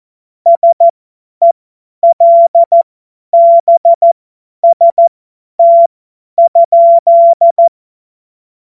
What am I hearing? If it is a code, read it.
Morse code, heard as a keyed tone, about 14 wpm, spaced as for 10 wpm, 680 Hz. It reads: SELBST?
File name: Morsekurs / Witze / Witz 04. Juli 2023 50 F70